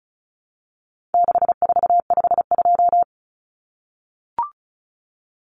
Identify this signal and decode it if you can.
Morse code, heard as a keyed tone, about 34 wpm, 700 Hz. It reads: 6452
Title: Common Cut Numbers - 4-Digit 35wpm